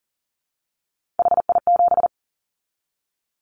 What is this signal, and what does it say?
Morse code, heard as a keyed tone, about 40 wpm, 690 Hz. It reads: HI7
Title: Callsign Prefixes for Top 100 DXCC - 40wpm